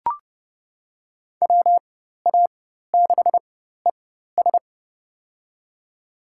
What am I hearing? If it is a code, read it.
Morse code, heard as a keyed tone, about 30 wpm, spaced as for 14 wpm, 700 Hz. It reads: WA6ES
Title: USA Call Signs - 5 Characters - 4x Character Spacing - 30wpm